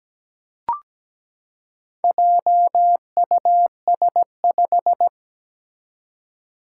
Morse code, heard as a keyed tone, about 17 wpm, 700 Hz. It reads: JUS5
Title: Four Challenging Letter-Number Combinations 17wpm